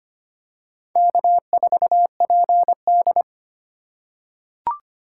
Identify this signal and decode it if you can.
Morse code, heard as a keyed tone, about 25 wpm, 700 Hz. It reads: K4PD
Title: USA Call Signs - 4 Characters - 4x Character Spacing - 25wpm